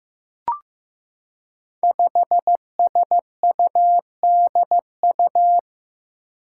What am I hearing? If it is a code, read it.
Morse code, heard as a keyed tone, about 15 wpm, 700 Hz. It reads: 5SUDU
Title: Five Challenging Letter-Number Combinations 15wpm